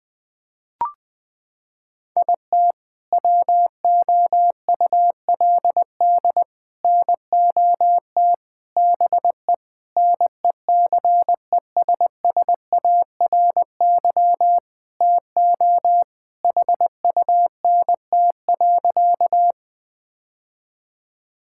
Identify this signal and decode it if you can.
Morse code, heard as a keyed tone, about 21 wpm, 700 Hz. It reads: IT WOULD NOT BE NECESSARY TO HUNT.
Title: Sentences from Top 1600 Words 20wpm